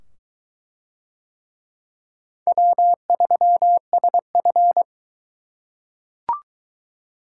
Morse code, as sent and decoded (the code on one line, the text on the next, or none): .-- ...-- ... ..-.
W3SF